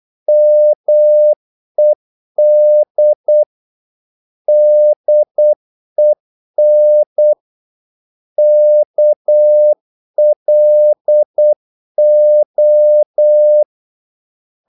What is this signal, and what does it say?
Morse code, heard as a keyed tone, about 8 wpm, 600 Hz. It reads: MED DEN KLO
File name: Korte ord DK 8wpm